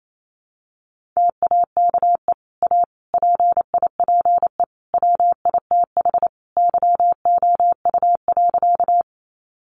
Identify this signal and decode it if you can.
Morse code, heard as a keyed tone, about 28 wpm, 700 Hz. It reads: TAKE A PIPE WITH YOU.
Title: Sentences from Top 1500 Words 28wpm